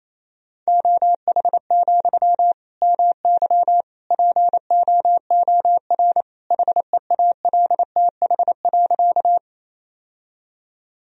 Morse code, heard as a keyed tone, about 28 wpm, 700 Hz. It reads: OH, MY POOR HEALTH.
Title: Sentences from Top 1000 Words - Review Part 3 - 28wpm